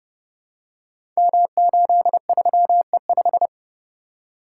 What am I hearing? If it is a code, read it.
Morse code, heard as a keyed tone, about 30 wpm, 700 Hz. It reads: M83E5
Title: Code Groups - 5 Letter:Numbers - 30wpm